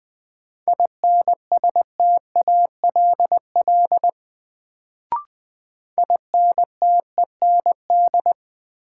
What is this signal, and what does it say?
Morse code, heard as a keyed tone, about 20 wpm, 700 Hz. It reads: INSTALL INTEND
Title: Sentences from Top 1600 Words - New Word Drill Alt - 20wpm